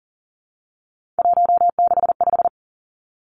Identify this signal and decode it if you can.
Morse code, heard as a keyed tone, about 40 wpm, 700 Hz. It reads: 165
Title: Three Challenging Letter-Number Combinations 40wpm